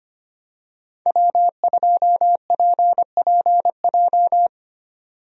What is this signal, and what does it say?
Morse code, heard as a keyed tone, about 25 wpm, 700 Hz. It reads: W2PPJ
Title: USA Call Signs - 5 Characters - 25wpm